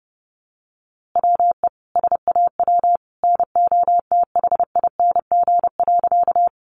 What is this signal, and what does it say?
Morse code, heard as a keyed tone, about 30 wpm, 700 Hz. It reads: WE SAW NOTHING.
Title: A Sound Of Thunder 30wpm